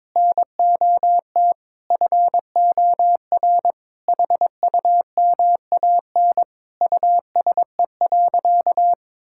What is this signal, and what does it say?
Morse code, heard as a keyed tone, about 22 wpm, 700 Hz. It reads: NOT FOR HUMAN USE.
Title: Sentences from Top 1000 Words - Review Part 6 - 22wpm